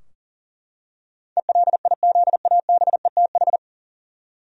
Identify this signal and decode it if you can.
Morse code, heard as a keyed tone, about 40 wpm, 700 Hz. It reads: ELIZABETH
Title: USA Common Names - 40wpm